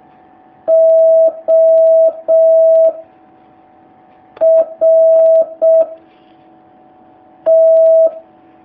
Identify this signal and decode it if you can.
Morse code, heard as a keyed tone, about 6 wpm, 650 Hz. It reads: O R T
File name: jh8cbh / 2019cw